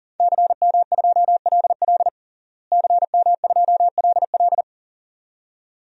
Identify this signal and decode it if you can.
Morse code, heard as a keyed tone, about 40 wpm, 700 Hz. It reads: CM2LL CM2LL
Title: DXCC-100 Call Signs - 5 Characters - Repeated 2x - 40wpm